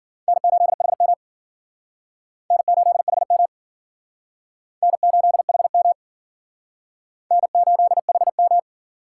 Morse code, heard as a keyed tone, about 51 wpm, 700 Hz. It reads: N8HM N8HM N8HM N8HM